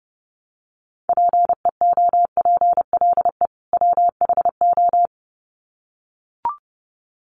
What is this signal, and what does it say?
Morse code, heard as a keyed tone, about 29 wpm, 700 Hz. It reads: PEOPLE WHO
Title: Sets of 2 Words 30wpm